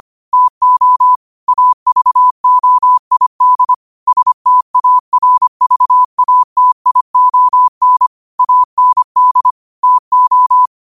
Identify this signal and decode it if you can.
Morse code, heard as a keyed tone, about 25 wpm, 1000 Hz. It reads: TO AVOID STARVATION AND TO
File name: qotdfiles / 25wpm